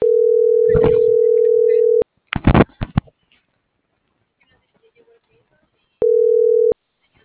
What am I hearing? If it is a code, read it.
Ringback tone (North America), 440 + 480 Hz.
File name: CENTRAL TERRESTRE / out lucero.sky